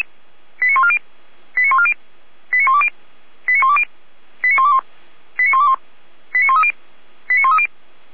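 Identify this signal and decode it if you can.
Five-tone selective call (ZVEI), heard as quick runs of five tones, tones of about 70 ms. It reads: 89140 89140 89120 89120 89121 89121 89130 89130
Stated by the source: Five-tone ZVEI selective call received by SDR.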